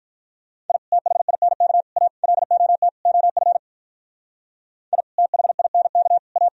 Morse code, heard as a keyed tone, about 45 wpm, 700 Hz. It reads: I THINK A LOT OF I THINK A